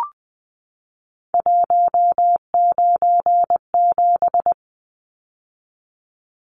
Morse code, heard as a keyed tone, about 20 wpm, 700 Hz. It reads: 197
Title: Numbers - 3-Digit 20wpm